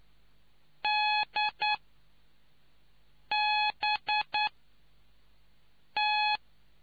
Morse code, heard as a keyed tone, about 9 wpm, spaced as for 4 wpm, 840 Hz. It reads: DBT